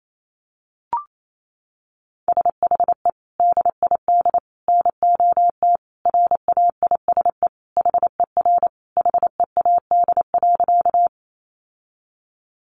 Morse code, heard as a keyed tone, about 28 wpm, 700 Hz. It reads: SHE DID NOT RAISE HER HEAD.
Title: Sentences from Top 1400 Words 28wpm